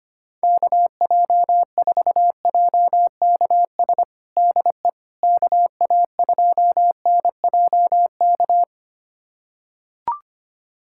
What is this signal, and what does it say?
Morse code, heard as a keyed tone, about 25 wpm, 700 Hz. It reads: KJ4JKS DE KA2NJK
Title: USA Callsign DE Callsign - 6 Characters - 25wpm